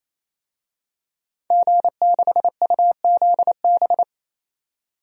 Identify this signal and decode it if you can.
Morse code, heard as a keyed tone, about 28 wpm, 700 Hz. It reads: G6UZB
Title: DXCC-100 Call Signs - 5 Characters - 28wpm